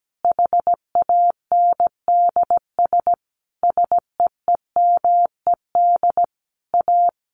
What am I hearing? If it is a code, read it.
Morse code, heard as a keyed tone, about 17 wpm, 700 Hz. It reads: HANDS SEEMED A
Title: A Sound Of Thunder 17wpm